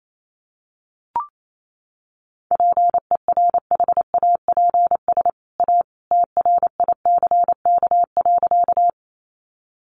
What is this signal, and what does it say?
Morse code, heard as a keyed tone, about 28 wpm, 700 Hz. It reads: PERHAPS A TRICK.
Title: Sentences from Top 1500 Words 28wpm